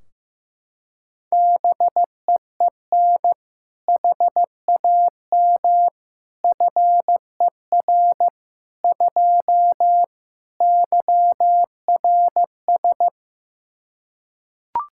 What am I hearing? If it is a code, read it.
Morse code, heard as a keyed tone, about 15 wpm, 700 Hz. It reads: BEEN HAM FER 2 YRS
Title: Practice QSO - Exchange 3 - More Info - 15wpm